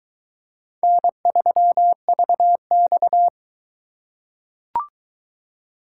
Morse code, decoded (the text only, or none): N3VX